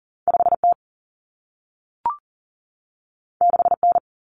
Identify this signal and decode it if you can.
Morse code, heard as a keyed tone, about 40 wpm, 700 Hz. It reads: 5T 6N
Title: Common Cut Numbers - 2-Digit 40wpm